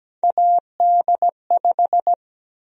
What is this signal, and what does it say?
Morse code, heard as a keyed tone, about 17 wpm, 700 Hz. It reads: AD5
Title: USA Call Sign Prefixes - Repeated 2x - 17wpm